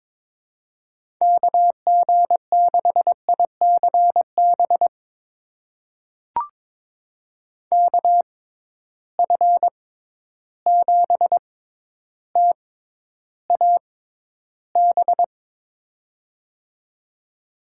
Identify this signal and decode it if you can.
Morse code, heard as a keyed tone, about 22 wpm, spaced as for 11 wpm, 700 Hz. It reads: KG6ICB KF7TAB
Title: USA Call Signs - 6 Characters - 6x Character Spacing - 22wpm